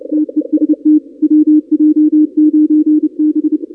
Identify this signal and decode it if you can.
Morse code, heard as a keyed tone, about 30 wpm, 310 Hz. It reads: TEST WJ9B
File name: contests / 2019-cw-wpx-kt5j-k5pi- -2330 0-31